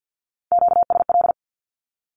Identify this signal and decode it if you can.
Morse code, heard as a keyed tone, about 48 wpm, 700 Hz. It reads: QSL
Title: Q-Codes 50wpm